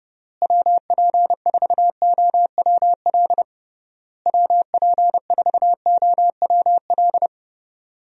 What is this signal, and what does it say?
Morse code, heard as a keyed tone, about 31 wpm, 700 Hz. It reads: WP4OWL WP4OWL
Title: USA Call Signs - 6 Characters - Repeated 3x - 30wpm